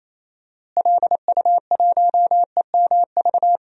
Morse code, heard as a keyed tone, about 28 wpm, 700 Hz. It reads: LU1EMV
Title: DXCC-100 Call Signs - 6 Characters - Repeated 2x - 28wpm